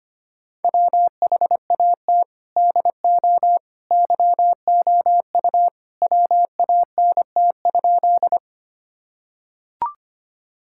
Morse code, heard as a keyed tone, about 25 wpm, 700 Hz. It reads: WHAT DO YOU WANT?
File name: practice / 25wpm / Sentences with 2-Word Phrases - 25wpm